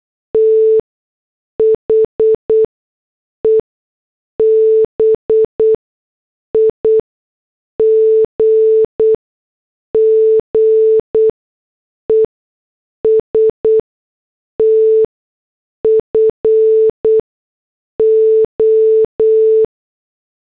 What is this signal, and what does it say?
Morse code, heard as a keyed tone, about 8 wpm, spaced as for 6 wpm, 440 Hz. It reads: THE BIGGEST FO